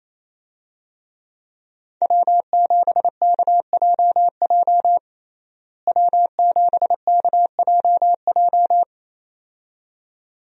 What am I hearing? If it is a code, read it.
Morse code, heard as a keyed tone, about 28 wpm, 700 Hz. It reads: W7KJJ W7KJJ
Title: USA Call Signs - 5 Characters - Repeated 2x - 28wpm